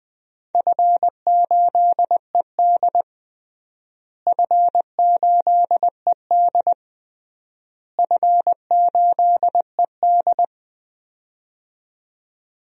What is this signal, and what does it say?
Morse code, heard as a keyed tone, about 19 wpm, 700 Hz. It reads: F8ED F8ED F8ED
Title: DXCC-100 Call Signs - 4 Characters - Repeated 3x - 20wpm